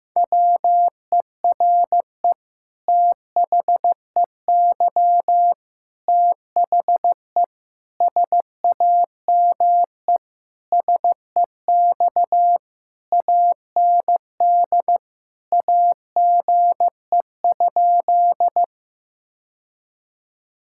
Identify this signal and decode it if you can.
Morse code, heard as a keyed tone, about 15 wpm, 700 Hz. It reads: WERE THEY THE SAME SEX AND AGE?
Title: Sentences from Top 1600 Words 15wpm